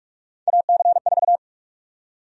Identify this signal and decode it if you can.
Morse code, heard as a keyed tone, about 46 wpm, 700 Hz. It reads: AC4
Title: USA Call Sign Prefixes - 45wpm